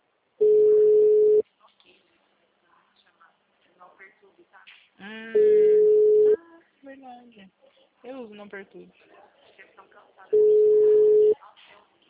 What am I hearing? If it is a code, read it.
Ringback tone (Europe, 425 Hz), with speech.